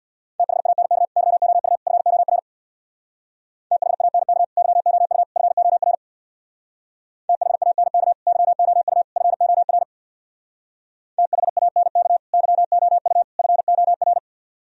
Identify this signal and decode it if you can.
Morse code, heard as a keyed tone, about 62 wpm, 700 Hz. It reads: THANK YOU FOR THANK YOU FOR THANK YOU FOR THANK YOU FOR
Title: Speed-Racing - Sets of 3 Words 50wpm